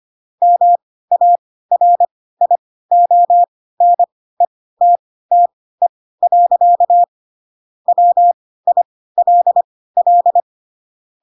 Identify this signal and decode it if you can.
Morse code, heard as a keyed tone, about 25 wpm, spaced as for 16 wpm, 700 Hz. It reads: MARIONETTE. WILL
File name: Pinocchio / Pinocchio ch2 10 25 ik1hge